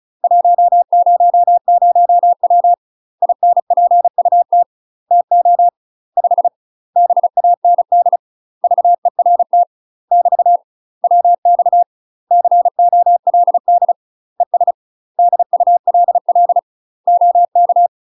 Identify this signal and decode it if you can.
Morse code, heard as a keyed tone, about 34 wpm, 700 Hz. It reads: 100W INPUT TO 5 BAND VERT <BT> WX COLD ES DULL OK
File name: UK MASTER QSOs TRAINING / 35WPM / UK MASTER QSOs 0025